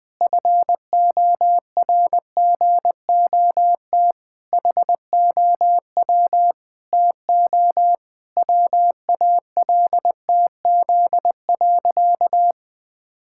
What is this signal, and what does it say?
Morse code, heard as a keyed tone, about 20 wpm, 700 Hz. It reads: FORGOT HOW TO WALTZ.
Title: Pangram Challenge v2 - 20wpm